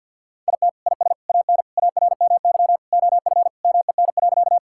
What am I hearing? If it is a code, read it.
Morse code, heard as a keyed tone, about 50 wpm, 700 Hz. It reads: IT IS AN ARMY OF MEN.
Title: Sentences from Top 1000 Words - Review Part 5 - 50wpm